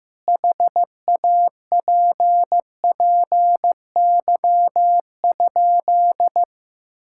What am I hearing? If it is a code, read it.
Morse code, heard as a keyed tone, about 16 wpm, 700 Hz. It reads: HAPPY?
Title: Sentences from Top 1000 Words - Review Part 5 - 15wpm